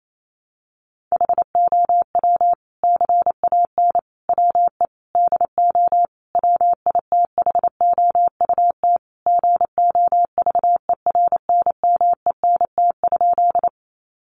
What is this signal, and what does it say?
Morse code, heard as a keyed tone, about 28 wpm, 700 Hz. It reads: HOW CAN WE DO WITHOUT GOVERNMENT?
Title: Sentences from Top 600 Words 28wpm